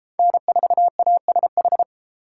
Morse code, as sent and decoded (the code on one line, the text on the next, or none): -. ....- .- ... ....
N4ASH